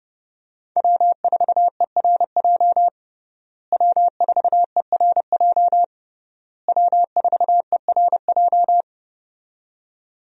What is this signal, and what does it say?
Morse code, heard as a keyed tone, about 32 wpm, 700 Hz. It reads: W4ERJ W4ERJ W4ERJ
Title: USA Call Signs - 5 Characters - Repeated 3x - 30wpm